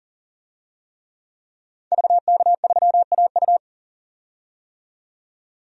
Morse code, heard as a keyed tone, about 40 wpm, 700 Hz. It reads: VK3AU
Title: DXCC-100 Call Signs - 5 Characters - 40wpm